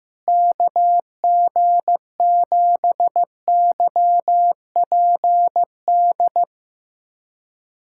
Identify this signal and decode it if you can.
Morse code, heard as a keyed tone, about 16 wpm, 700 Hz. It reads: KG7YPD
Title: USA Callsign DE Callsign 15wpm